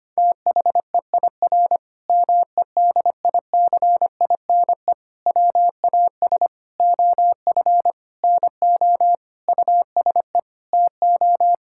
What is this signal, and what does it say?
Morse code, heard as a keyed tone, about 25 wpm, 700 Hz. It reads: THEIR MEDICINE WAS OF NO USE TO
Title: Sentences from Top 1000 Words - Review Part 6 - 25wpm